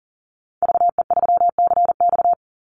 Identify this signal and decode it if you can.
Morse code, heard as a keyed tone, about 40 wpm, 700 Hz. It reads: VE3CX
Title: Call Signs Repeated 2x - 40wpm